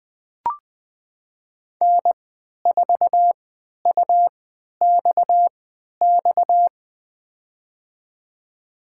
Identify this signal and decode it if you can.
Morse code, heard as a keyed tone, about 21 wpm, spaced as for 11 wpm, 700 Hz. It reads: N4UXX
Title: USA Call Signs - 5 Characters - 3x Character Spacing - 20wpm